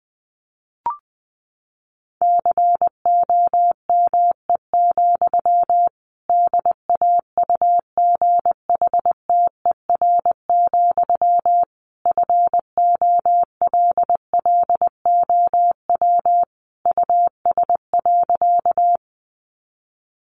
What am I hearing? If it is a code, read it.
Morse code, heard as a keyed tone, about 20 wpm, 700 Hz. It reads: COME, DAUGHTER, FOLLOW US.
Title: Sentences from Top 1500 Words 20wpm